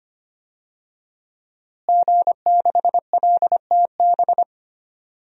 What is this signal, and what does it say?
Morse code, heard as a keyed tone, about 25 wpm, 700 Hz. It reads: G6LTB
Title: DXCC-100 Call Signs - 5 Characters - Repeated 2x - 25wpm